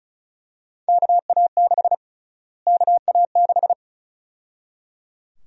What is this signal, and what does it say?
Morse code, heard as a keyed tone, about 34 wpm, 700 Hz. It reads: KA6 KA6
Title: USA Call Sign Prefixes - Repeated 2x - 35wpm